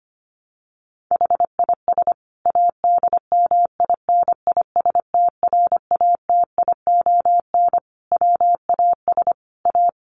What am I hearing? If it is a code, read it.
Morse code, heard as a keyed tone, about 25 wpm, 700 Hz. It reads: HIS ADMINISTRATION WAS A
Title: Sentences from Top 1300 Words 25wpm